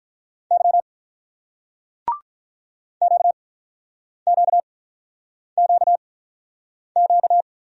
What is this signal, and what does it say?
Morse code, heard as a keyed tone, about 47 wpm, spaced as for 9 wpm, 700 Hz. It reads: Y QQQQ